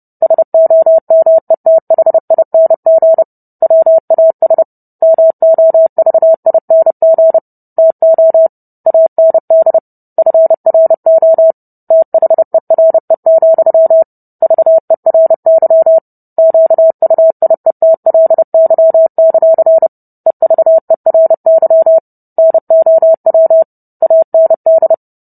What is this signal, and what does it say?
Morse code, heard as a keyed tone, about 30 wpm, 650 Hz. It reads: SOMETHING WAS MOVING TO AND FRO THERE, VERY QUIETLY; EVERY NOW AND